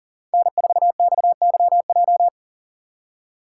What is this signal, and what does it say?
Morse code, heard as a keyed tone, about 40 wpm, 700 Hz. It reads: N4XYJ
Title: USA Call Signs - 5 Characters - 2x Character Spacing - 40wpm